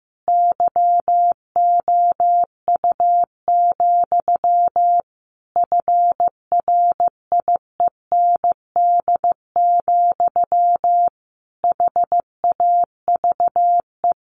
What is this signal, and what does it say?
Morse code, heard as a keyed tone, about 15 wpm, 700 Hz. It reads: YOU, FRIEND, HAVE
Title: A Sound Of Thunder 15wpm